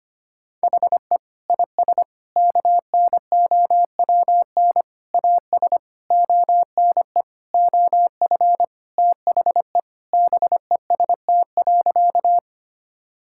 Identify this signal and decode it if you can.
Morse code, heard as a keyed tone, about 25 wpm, 700 Hz. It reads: HE IS KNOWN AS ONE OF THE BEST.
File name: practice / 25wpm / Sentences with 4-Word Phrases - 25wpm